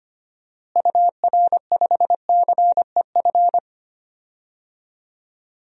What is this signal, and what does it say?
Morse code, heard as a keyed tone, about 25 wpm, 700 Hz. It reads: UR5CEF
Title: DXCC-100 Call Signs - 6 Characters - Repeated 3x - 25wpm